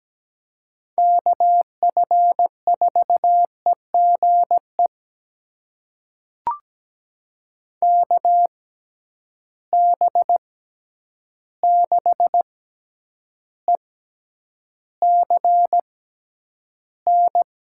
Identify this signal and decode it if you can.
Morse code, heard as a keyed tone, about 17 wpm, spaced as for 9 wpm, 700 Hz. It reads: KF4EGE KB6ECN
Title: USA Call Signs - 6 Characters - 6x Character Spacing - 17wpm